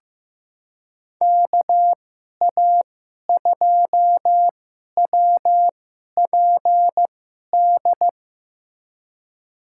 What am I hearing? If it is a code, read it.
Morse code, heard as a keyed tone, about 16 wpm, spaced as for 11 wpm, 700 Hz. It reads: KA2WPD
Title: USA Call Signs - 6 Characters - 2x Character Spacing - 15wpm